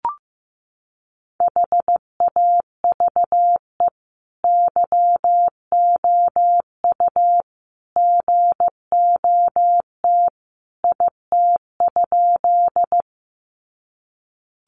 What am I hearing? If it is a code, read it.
Morse code, heard as a keyed tone, about 15 wpm, 700 Hz. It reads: HAVE YOU GOT IT?
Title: Sentences from Top 300 Words 15wpm